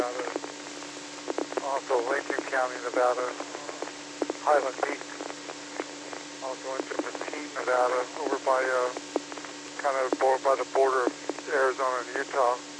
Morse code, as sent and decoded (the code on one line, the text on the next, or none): none